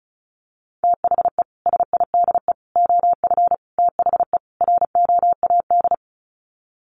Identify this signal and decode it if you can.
Morse code, heard as a keyed tone, about 34 wpm, 700 Hz. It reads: THE SIDE OF THE ROAD
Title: Sets of 5 Words 35wpm